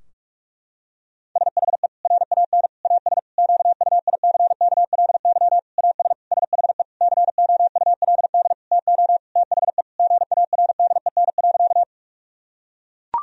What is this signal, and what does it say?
Morse code, heard as a keyed tone, about 41 wpm, 700 Hz. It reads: SHE RAN AS QUICKLY AS SHE COULD TO THE GARDEN.
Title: Sentences from Top 1000 Words - Review Part 4 - 45wpm